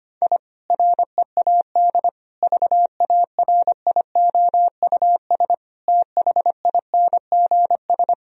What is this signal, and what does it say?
Morse code, heard as a keyed tone, about 25 wpm, 700 Hz. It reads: I READ VARIOUS THINGS
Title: Sentences from Top 1000 Words - Review Part 4 - 25wpm